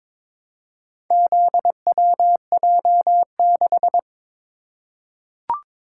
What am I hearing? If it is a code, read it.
Morse code, heard as a keyed tone, about 22 wpm, 700 Hz. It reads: ZWJ6
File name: practice / 22wpm / Four Letter-Number Combinations - 22wpm